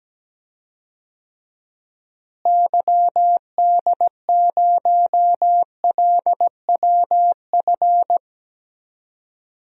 Morse code, heard as a keyed tone, about 17 wpm, 700 Hz. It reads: YD0LWF
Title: DXCC-100 Call Signs - 6 Characters - Repeated 3x - 17wpm